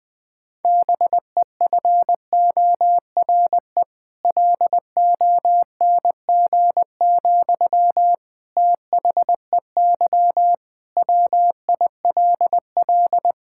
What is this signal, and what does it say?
Morse code, heard as a keyed tone, about 20 wpm, 700 Hz. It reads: BEFORE LONG, THEY WILL
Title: Sentences from Top 1000 Words - Review Part 6 - 20wpm